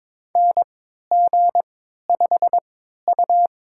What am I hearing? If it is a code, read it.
Morse code, heard as a keyed tone, about 22 wpm, spaced as for 13 wpm, 700 Hz. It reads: NG5U
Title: USA Call Signs - 3x Character Spacing - 22wpm